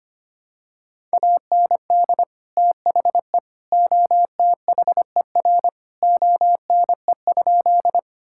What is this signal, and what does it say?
Morse code, heard as a keyed tone, about 25 wpm, 700 Hz. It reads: AND THE OTHER ONE?
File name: practice / 25wpm / Sentences with 3-Word Phrases - 25wpm